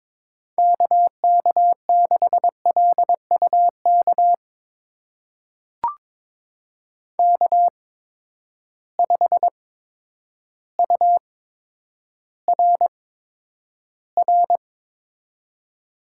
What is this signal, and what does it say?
Morse code, heard as a keyed tone, about 22 wpm, spaced as for 10 wpm, 700 Hz. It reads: KK6LUK K5URR
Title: USA Call Signs - 8x Character Spacing - 22wpm